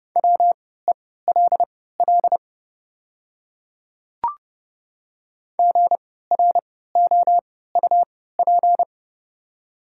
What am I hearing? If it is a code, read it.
Morse code, heard as a keyed tone, about 30 wpm, spaced as for 17 wpm, 700 Hz. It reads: WELL GROUP
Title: 3x Character Spacing - Top 500 Words - 30wpm